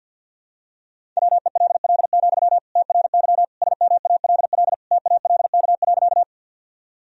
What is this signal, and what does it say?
Morse code, heard as a keyed tone, about 52 wpm, 700 Hz. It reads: WELL, TRY SMALL TALK.
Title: Sentences from Top 1000 Words - Review Part 1 - 50wpm